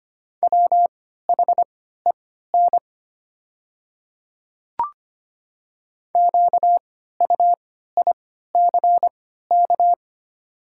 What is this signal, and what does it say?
Morse code, heard as a keyed tone, about 25 wpm, spaced as for 14 wpm, 700 Hz. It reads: WHEN QUICK
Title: 3x Character Spacing - Top 500 Words - 25wpm